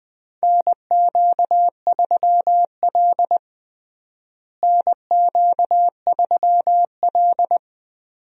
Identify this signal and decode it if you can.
Morse code, heard as a keyed tone, about 20 wpm, 700 Hz. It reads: NQ3L NQ3L
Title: USA Call Signs - 4 Characters - Repeated 2x - 20wpm